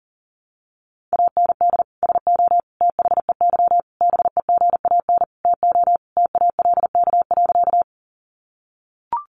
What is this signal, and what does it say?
Morse code, heard as a keyed tone, about 40 wpm, 700 Hz. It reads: AND SO THEY BEGAN TO TALK.
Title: Sentences from Top 400 Words 40wpm